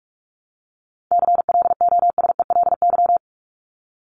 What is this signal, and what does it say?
Morse code, heard as a keyed tone, about 44 wpm, 700 Hz. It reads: CLOSELY
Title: Top 1251-1500 Words 45wpm